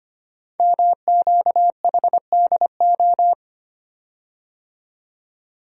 Morse code, heard as a keyed tone, about 25 wpm, 700 Hz. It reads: MQHDO
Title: Code Groups - 5 Letters - 25wpm